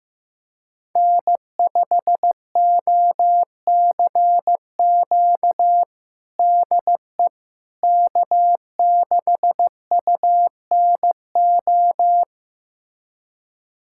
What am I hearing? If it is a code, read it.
Morse code, heard as a keyed tone, about 15 wpm, 700 Hz. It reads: N5OCQ DE K6UNO
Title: USA Callsign DE Callsign - 5 Characters - 15wpm